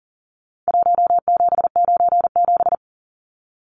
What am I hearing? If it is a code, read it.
Morse code, heard as a keyed tone, about 40 wpm, 700 Hz. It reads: 1797
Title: Numbers - Contest Style - 4-Digit 40wpm